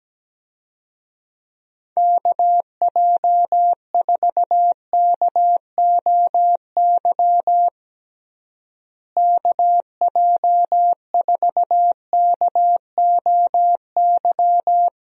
Morse code, heard as a keyed tone, about 17 wpm, 700 Hz. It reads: KJ4KOY KJ4KOY
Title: USA Call Signs - 6 Characters - Repeated 3x - 17wpm